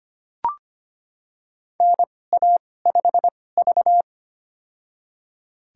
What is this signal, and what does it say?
Morse code, heard as a keyed tone, about 25 wpm, spaced as for 18 wpm, 700 Hz. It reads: NA5V
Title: USA Call Signs - 4 Characters - 2x Character Spacing - 25wpm